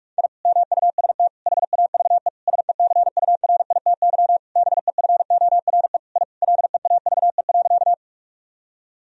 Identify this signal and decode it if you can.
Morse code, heard as a keyed tone, about 46 wpm, 700 Hz. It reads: I MUST HAVE SECURITY BEFORE I LEAVE.
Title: Sentences from Top 1000 Words - Review Part 4 - 45wpm